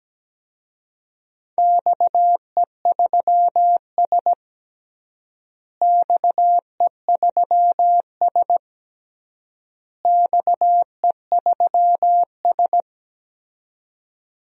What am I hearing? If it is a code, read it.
Morse code, heard as a keyed tone, about 17 wpm, 700 Hz. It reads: XE3S XE3S XE3S
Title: DXCC-100 Call Signs - 4 Characters - Repeated 3x - 17wpm